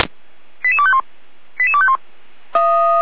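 Five-tone selective call (ZVEI), heard as quick runs of five tones, tones of about 70 ms. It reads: 88251 88251
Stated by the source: ZVEI five-tone selective call received on SDR.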